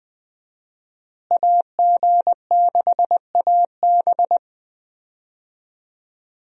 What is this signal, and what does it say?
Morse code, heard as a keyed tone, about 20 wpm, 700 Hz. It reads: AG6AB
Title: USA Call Signs - 5 Characters - Repeated 2x - 20wpm